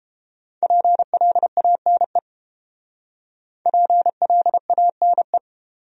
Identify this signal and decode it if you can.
Morse code, heard as a keyed tone, about 30 wpm, 700 Hz. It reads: PLANE PLANE